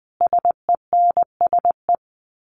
Morse code, heard as a keyed tone, about 20 wpm, 700 Hz. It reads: SENSE
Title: Sets of 5 Words Encore2 20wpm